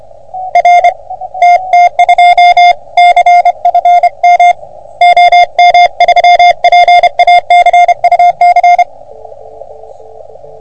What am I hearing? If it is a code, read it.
Morse code, heard as a keyed tone, about 28 wpm, 720 Hz. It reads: R TT2 CFM OM3PA CUC
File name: OM3PA PILUP CUC 388